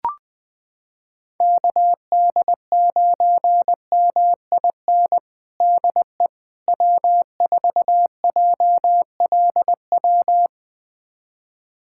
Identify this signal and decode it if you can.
Morse code, heard as a keyed tone, about 21 wpm, 700 Hz. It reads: KD9MIN DE W4JLW
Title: USA Callsign DE Callsign 20wpm